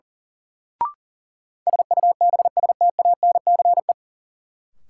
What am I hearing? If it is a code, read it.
Morse code, heard as a keyed tone, about 37 wpm, 700 Hz. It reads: SUBSTANCE
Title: Sentences from Top 1500 Words - New Word Drill - 40wpm